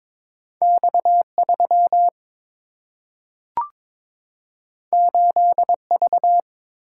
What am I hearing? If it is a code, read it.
Morse code, heard as a keyed tone, about 22 wpm, 700 Hz. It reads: X 3 8 V
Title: Single-Letter Single-Number 22wpm